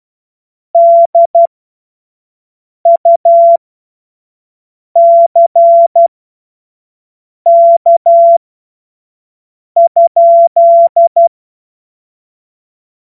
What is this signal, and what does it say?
Morse code, heard as a keyed tone, about 12 wpm, spaced as for 5 wpm, 680 Hz. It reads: DUCK?
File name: Morsekurs / Witze / Witz 07. Juni 2022 25 F70